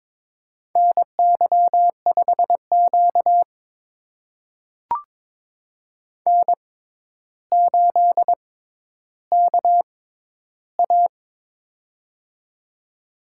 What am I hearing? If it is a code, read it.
Morse code, heard as a keyed tone, about 22 wpm, spaced as for 11 wpm, 700 Hz. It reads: NY5Q N8KA
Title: USA Call Signs - 4 Characters - 6x Character Spacing - 22wpm